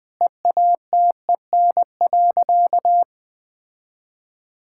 Morse code, heard as a keyed tone, about 20 wpm, 700 Hz. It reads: EATEN.
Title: Sentences with 2-Word Phrases - 20wpm